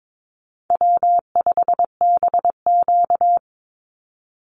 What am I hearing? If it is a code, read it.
Morse code, heard as a keyed tone, about 22 wpm, 700 Hz. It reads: W5BQ
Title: Call Signs Repeated 3x - 22wpm